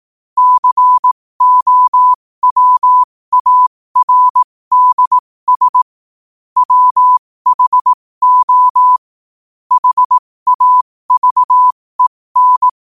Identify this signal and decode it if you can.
Morse code, heard as a keyed tone, about 18 wpm, spaced as for 15 wpm, 1000 Hz. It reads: COWARDS WHO HAVEN
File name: qotdfiles / 15wpm